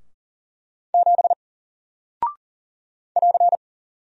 Morse code, heard as a keyed tone, about 40 wpm, 700 Hz. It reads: 7 <AR>
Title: Single Character (Letter:Number:Punctuation:Prosign) 40wpm